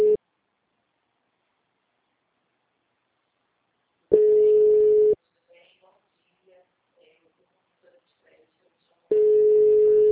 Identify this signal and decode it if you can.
Ringback tone (Europe, 425 Hz), with speech.